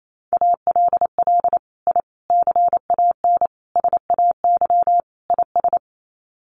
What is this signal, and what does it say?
Morse code, heard as a keyed tone, about 28 wpm, 700 Hz. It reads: ALL I CAN SAY IS
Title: Sets of 5 Words Encore2 28wpm